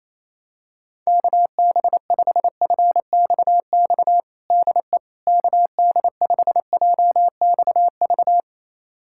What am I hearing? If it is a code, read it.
Morse code, heard as a keyed tone, about 28 wpm, 700 Hz. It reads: KB5FXX DE KD5JXV
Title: USA Callsign DE Callsign - 6 Characters - 28wpm